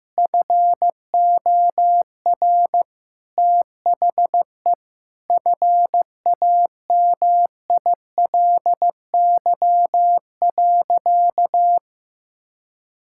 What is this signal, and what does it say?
Morse code, heard as a keyed tone, about 15 wpm, 700 Hz. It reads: FOR THE FAMILY.
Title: Sentences from Top 1000 Words - Review Part 5 - 15wpm